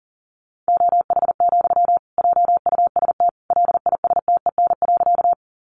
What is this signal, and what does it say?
Morse code, heard as a keyed tone, about 40 wpm, 700 Hz. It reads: OH, JUST LISTEN.
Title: Sentences from Top 500 Words 40wpm